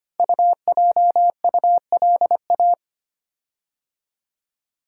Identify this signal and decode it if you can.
Morse code, heard as a keyed tone, about 25 wpm, 700 Hz. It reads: UJULA
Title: Code Groups - 5 Letters - 25wpm